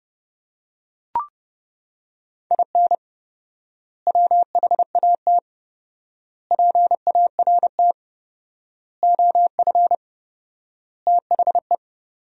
Morse code, heard as a keyed tone, about 29 wpm, 700 Hz. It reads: IN WHAT PART OF THE
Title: 4x Word-Spacing - Sentences from Top 200 Words - 30wpm